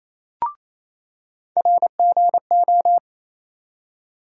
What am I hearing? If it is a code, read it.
Morse code, heard as a keyed tone, about 28 wpm, 700 Hz. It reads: RGO
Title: USA Call Sign Postfixes - 3 Letters 28wpm